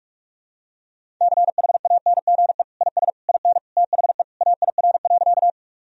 Morse code, heard as a keyed tone, about 46 wpm, 700 Hz. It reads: CHANGE IS IN THE AIR.
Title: Sentences from Top 1000 Words - Review Part 1 - 45wpm